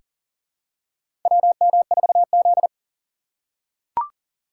Morse code, heard as a keyed tone, about 44 wpm, 700 Hz. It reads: WM4Z
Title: USA Call Signs - 4 Characters - 8x Character Spacing - 40wpm